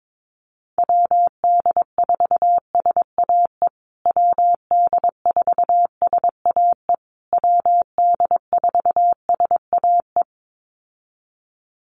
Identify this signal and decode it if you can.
Morse code, heard as a keyed tone, about 22 wpm, 700 Hz. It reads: WD4SAE WD4SAE WD4SAE
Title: USA Call Signs Repeated 3x 22wpm